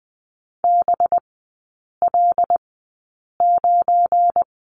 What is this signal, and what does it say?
Morse code, heard as a keyed tone, about 20 wpm, spaced as for 8 wpm, 700 Hz. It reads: BL9
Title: 184 Lesson 30e - Letter b - Three Characters 20wpm